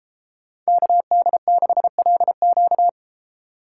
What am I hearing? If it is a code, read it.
Morse code, heard as a keyed tone, about 33 wpm, 700 Hz. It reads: KD6LQ